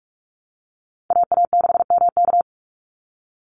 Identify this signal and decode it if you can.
Morse code, heard as a keyed tone, about 44 wpm, 700 Hz. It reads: AA6MK
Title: Call Signs Repeated 2x - 45wpm